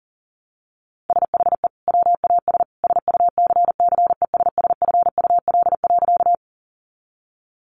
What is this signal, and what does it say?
Morse code, heard as a keyed tone, about 40 wpm, 700 Hz. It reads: SHE WAS SUCCESSFUL.
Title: Sentences from Top 900 Words 40wpm